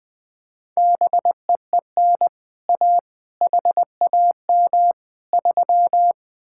Morse code, heard as a keyed tone, about 20 wpm, 700 Hz. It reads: BEEN A HAM 3
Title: Speed-Racing Common QSO Phrases - 17wpm